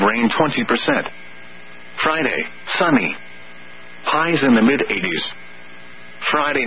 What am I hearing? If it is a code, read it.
Emergency Alert System (SAME) tones (header bursts), heard as harsh buzzing bursts, with speech. It is heard from 1.2 to 6.1 s.